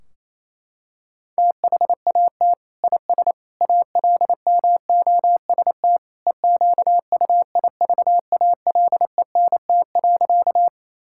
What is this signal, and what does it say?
Morse code, heard as a keyed tone, about 28 wpm, 700 Hz. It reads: THAT IS ALMOST EQUIVALENT.
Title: Sentences from Top 1600 Words 28wpm